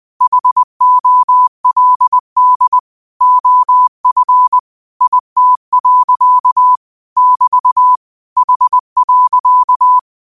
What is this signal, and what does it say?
Morse code, heard as a keyed tone, about 20 wpm, 1000 Hz. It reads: HOLD OF IT. <BT> H.
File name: qotdfiles / 20wpm